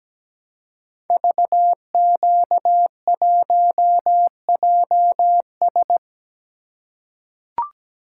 Morse code, heard as a keyed tone, about 17 wpm, 700 Hz. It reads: VQ1JS
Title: Code Groups - 5 Letter:Numbers - 17wpm